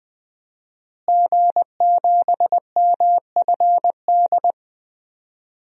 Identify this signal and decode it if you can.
Morse code, heard as a keyed tone, about 20 wpm, 700 Hz. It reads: G7MFD
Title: DXCC-100 Call Signs - 5 Characters - Repeated 3x - 20wpm